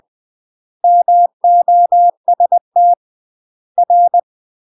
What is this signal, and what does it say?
Morse code, heard as a keyed tone, about 20 wpm, 700 Hz. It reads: MOST R
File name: UK MASTER QSOs TRAINING / 20WPM / UK MASTER QSOs 0066